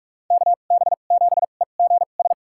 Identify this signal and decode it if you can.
Morse code, heard as a keyed tone, about 42 wpm, spaced as for 31 wpm, 700 Hz. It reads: KB7EGS
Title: USA Call Signs - 2x Character Spacing - 45wpm